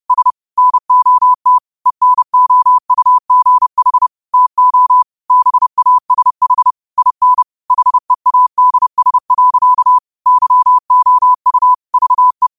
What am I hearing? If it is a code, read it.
Morse code, heard as a keyed tone, about 30 wpm, 1000 Hz. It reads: S NOT ENOUGH TO BASH IN HEADS. YOU VE
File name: qotdfiles / 30wpm